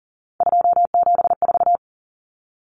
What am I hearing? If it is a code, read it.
Morse code, heard as a keyed tone, about 40 wpm, 700 Hz. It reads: 274
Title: Numbers - 3-Digit 40wpm